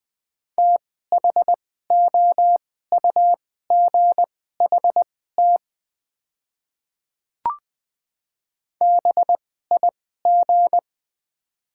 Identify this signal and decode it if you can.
Morse code, heard as a keyed tone, about 20 wpm, spaced as for 14 wpm, 700 Hz. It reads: THOUGHT BIG
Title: 2x Character Spacing - Top 500 Words - 20wpm